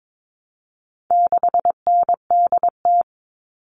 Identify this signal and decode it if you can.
Morse code, heard as a keyed tone, about 22 wpm, 700 Hz. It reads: 6NDT
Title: All Cut Numbers - 4-Digit 22wpm